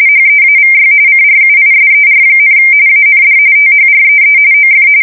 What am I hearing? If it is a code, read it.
RTTY, about 45.45 baud. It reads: HEAVY CRUISER, AND THREE OF
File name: fsk170 / USS-New-Jersey